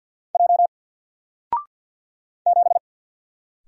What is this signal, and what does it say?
Morse code, heard as a keyed tone, about 48 wpm, 700 Hz. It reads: J 7
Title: Single Letter-Number 50wpm